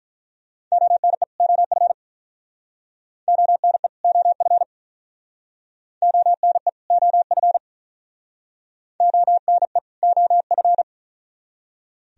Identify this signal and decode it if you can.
Morse code, heard as a keyed tone, about 48 wpm, 700 Hz. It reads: ONE OF ONE OF ONE OF ONE OF